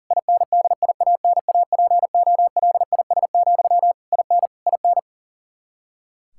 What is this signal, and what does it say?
Morse code, heard as a keyed tone, about 40 wpm, 700 Hz. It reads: INDIANAPOLIS, IN IN
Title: Practice QSO - Exchange 1 - Beginning - 40wpm